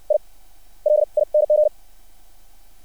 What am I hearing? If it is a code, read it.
Morse code, heard as a keyed tone, about 60 wpm, spaced as for 27 wpm, 600 Hz. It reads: ILING
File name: archived / CRSnet